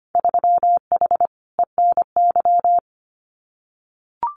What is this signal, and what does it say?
Morse code, heard as a keyed tone, about 25 wpm, 700 Hz. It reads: 3H ENY
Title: Winter Field Day Exchanges Encore 25wpm